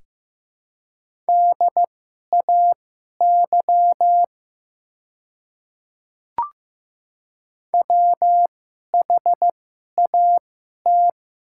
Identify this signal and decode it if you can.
Morse code, heard as a keyed tone, about 15 wpm, spaced as for 11 wpm, 690 Hz. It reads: DAY WHAT
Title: 2x Character Spacing - Top 100 Words - 15wpm